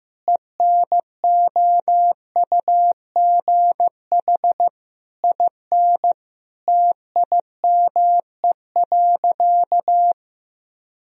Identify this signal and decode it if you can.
Morse code, heard as a keyed tone, about 15 wpm, 700 Hz. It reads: ENOUGH IN TIME.
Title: Sentences from Top 1000 Words - Review Part 2 - 15wpm